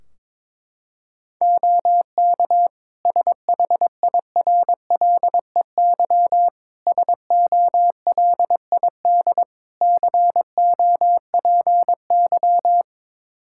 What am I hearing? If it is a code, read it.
Morse code, heard as a keyed tone, about 22 wpm, 700 Hz. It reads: OK SHIRLEY SOLID COPY
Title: Practice QSO - Exchange 3 - More Info - 22wpm